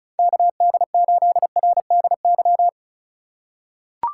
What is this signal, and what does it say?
Morse code, heard as a keyed tone, about 36 wpm, 700 Hz. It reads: KD8RDY
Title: USA Call Signs - 4x Character Spacing - 35wpm